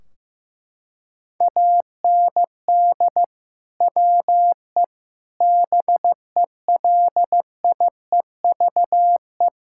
Morse code, heard as a keyed tone, about 15 wpm, 700 Hz. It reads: AND WE BELIEVE
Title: Sentences from Top 1600 Words 15wpm